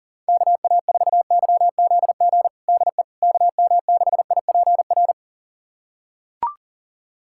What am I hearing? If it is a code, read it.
Morse code, heard as a keyed tone, about 40 wpm, 700 Hz. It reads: KA4YZG DE KM6IPR
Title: USA Callsign DE Callsign - 6 Characters - 40wpm